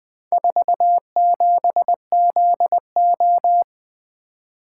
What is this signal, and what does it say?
Morse code, heard as a keyed tone, about 20 wpm, 700 Hz. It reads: 47ZO
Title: Four Letter-Number Combinations - 20wpm